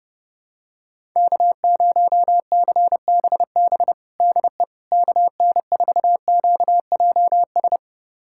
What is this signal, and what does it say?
Morse code, heard as a keyed tone, about 32 wpm, 700 Hz. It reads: K0CBB DE KN4QJS
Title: USA Callsign DE Callsign 30wpm